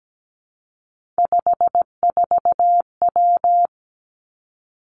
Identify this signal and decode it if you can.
Morse code, heard as a keyed tone, about 17 wpm, 700 Hz. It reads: 54W
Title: Three Challenging Letter-Number Combinations 17wpm